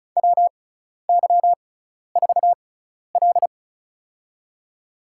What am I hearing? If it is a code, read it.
Morse code, heard as a keyed tone, about 36 wpm, spaced as for 12 wpm, 700 Hz. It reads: WY4L
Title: USA Call Signs - 4 Characters - 6x Character Spacing - 35wpm